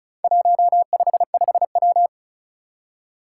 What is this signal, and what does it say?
Morse code, heard as a keyed tone, about 36 wpm, 700 Hz. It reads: 155W
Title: Four Challenging Letter-Number Combinations 35wpm